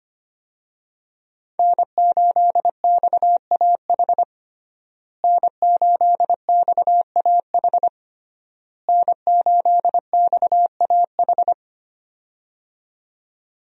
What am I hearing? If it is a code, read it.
Morse code, heard as a keyed tone, about 25 wpm, 700 Hz. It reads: N8XAH N8XAH N8XAH
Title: USA Call Signs - 5 Characters - Repeated 3x - 25wpm